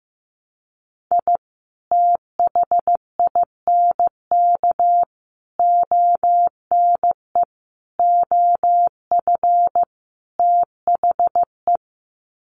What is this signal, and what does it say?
Morse code, heard as a keyed tone, about 15 wpm, 700 Hz. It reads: I THINK ONE OF THE
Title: Sets of 5 Words 15wpm